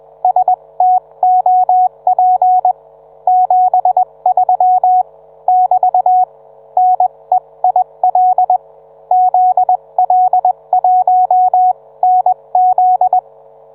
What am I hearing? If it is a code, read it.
Morse code, heard as a keyed tone, about 21 wpm, spaced as for 18 wpm, 740 Hz. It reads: STOP 73 <BT> NEIL ZL1NZ